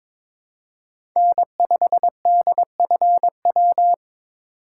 Morse code, heard as a keyed tone, about 22 wpm, 700 Hz. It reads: N5DFW
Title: USA Call Signs - 3x Character Spacing - 22wpm